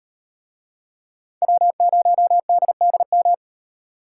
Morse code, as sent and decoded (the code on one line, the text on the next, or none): .-- ----- -.. -.. --
W0DDM